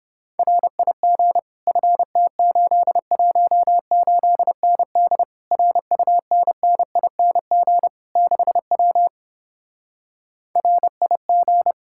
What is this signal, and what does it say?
Morse code, heard as a keyed tone, about 29 wpm, 700 Hz. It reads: RIG FT818ND RUNNING 6W RIG